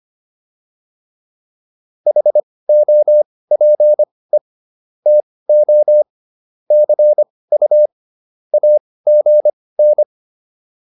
Morse code, heard as a keyed tone, about 25 wpm, spaced as for 18 wpm, 600 Hz. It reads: HOPE TO CU AGN